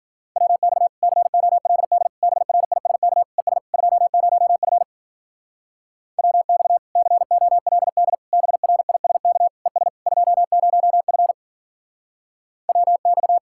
Morse code, heard as a keyed tone, about 56 wpm, 700 Hz. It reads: WX COLD BRISK ES 20F WX COLD BRISK ES 20F WX